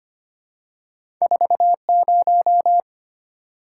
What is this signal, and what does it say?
Morse code, heard as a keyed tone, about 25 wpm, 700 Hz. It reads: 40
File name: Two Letter-Number Combinations - 25wpm